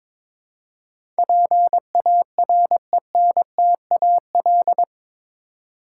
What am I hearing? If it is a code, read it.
Morse code, heard as a keyed tone, about 22 wpm, 700 Hz. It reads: PARENTAL
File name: practice / 22wpm / Eight Letter Words - 22wpm